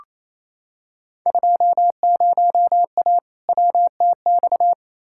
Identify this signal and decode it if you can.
Morse code, heard as a keyed tone, about 28 wpm, 700 Hz. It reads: 20A WTX
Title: Field Day Exchanges Encore - 28wpm